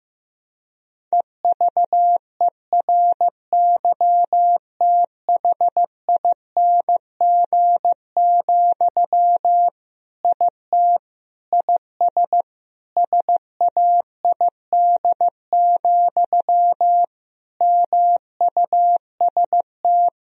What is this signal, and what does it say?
Morse code, heard as a keyed tone, about 15 wpm, 700 Hz. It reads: EVERYTHING, IT IS SAID, MUST
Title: Sentences from Top 1000 Words - Review Part 1 - 15wpm